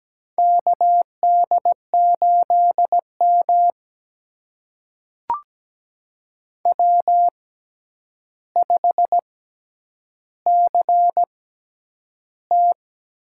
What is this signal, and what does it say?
Morse code, heard as a keyed tone, about 17 wpm, spaced as for 9 wpm, 700 Hz. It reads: KD8M W5CT
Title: USA Call Signs - 4 Characters - 6x Character Spacing - 17wpm